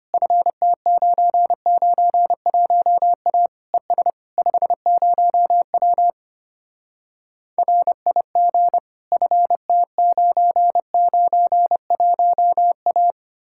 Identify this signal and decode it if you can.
Morse code, heard as a keyed tone, about 33 wpm, 700 Hz. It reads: FT991A ES 50W RIG FT991A